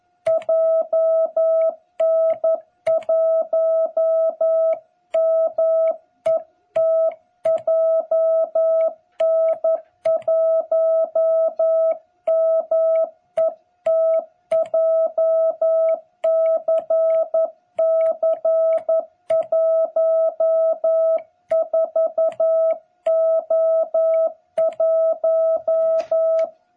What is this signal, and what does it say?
Morse code, heard as a keyed tone, about 11 wpm, 660 Hz. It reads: JN1MET JN1MET JCC14O1
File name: jh8cbh / 2021cw